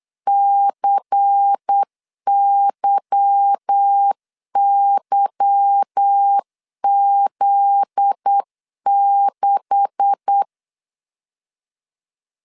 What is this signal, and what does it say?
Morse code, heard as a keyed tone, about 8 wpm, 800 Hz. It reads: CYYZ6
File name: CW / DET / 1 9 1500C S A 8wpm 5637